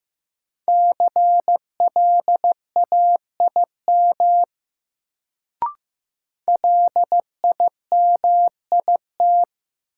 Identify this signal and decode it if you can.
Morse code, heard as a keyed tone, about 15 wpm, 700 Hz. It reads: CLAIM LIMIT
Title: Sentences from Top 1500 Words - New Word Drill Alt - 15wpm